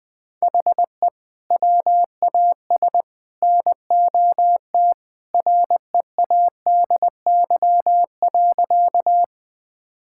Morse code, heard as a keyed tone, about 21 wpm, 700 Hz. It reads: HE WAS NOT READY.
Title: Sentences with 3-Word Phrases - 20wpm